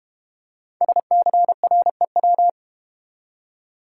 Morse code, heard as a keyed tone, about 33 wpm, 700 Hz. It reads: SCREW